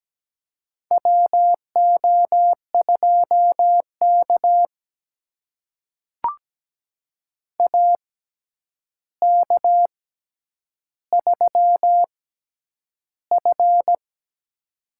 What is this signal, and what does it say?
Morse code, heard as a keyed tone, about 17 wpm, spaced as for 9 wpm, 700 Hz. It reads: WO2K AK3F
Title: USA Call Signs - 4 Characters - 6x Character Spacing - 17wpm